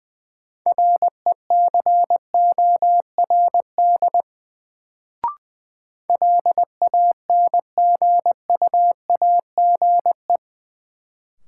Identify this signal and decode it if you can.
Morse code, heard as a keyed tone, about 20 wpm, 700 Hz. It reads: RECORD LANGUAGE
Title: Sentences from Top 500 Words - New Word Drill - 20wpm